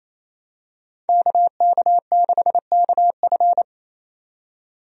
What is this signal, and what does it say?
Morse code, heard as a keyed tone, about 28 wpm, 700 Hz. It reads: KK6KF
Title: USA Call Signs - 5 Characters - 28wpm